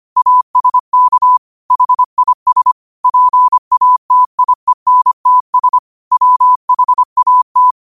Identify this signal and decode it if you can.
Morse code, heard as a keyed tone, about 25 wpm, 1000 Hz. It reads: ASK HIS PATIENTS WHAT